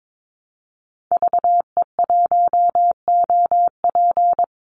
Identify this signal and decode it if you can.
Morse code, heard as a keyed tone, about 22 wpm, 700 Hz. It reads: VE1OP
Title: Call Signs Repeated 2x - 22wpm